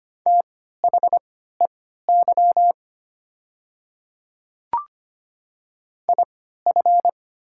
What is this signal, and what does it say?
Morse code, heard as a keyed tone, about 25 wpm, spaced as for 14 wpm, 700 Hz. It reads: THEY IF
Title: 3x Character Spacing - Top 100 Words - 25wpm